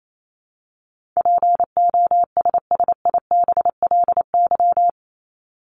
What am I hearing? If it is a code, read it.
Morse code, heard as a keyed tone, about 28 wpm, 700 Hz. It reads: POSSIBLY
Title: LY Suffix - Top 100 Words with Base Words 28wpm